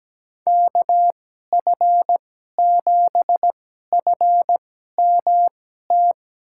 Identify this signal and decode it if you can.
Morse code, heard as a keyed tone, about 17 wpm, spaced as for 12 wpm, 700 Hz. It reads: KF7FMT
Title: USA Call Signs - 6 Characters - 2x Character Spacing - 17wpm